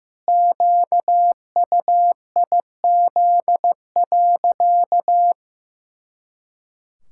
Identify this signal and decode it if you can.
Morse code, heard as a keyed tone, about 15 wpm, 700 Hz. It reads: QUIZ.
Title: Pangram Challenge v2 - 15wpm